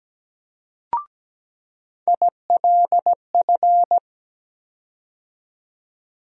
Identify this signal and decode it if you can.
Morse code, heard as a keyed tone, about 17 wpm, 700 Hz. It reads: ILF
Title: USA Call Sign Postfixes - 3 Letters 17wpm